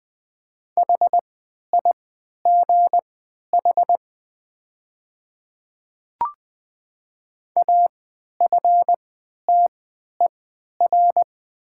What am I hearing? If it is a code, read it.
Morse code, heard as a keyed tone, about 20 wpm, spaced as for 11 wpm, 700 Hz. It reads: HIGH AFTER
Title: 3x Character Spacing - Top 200 Words - 20wpm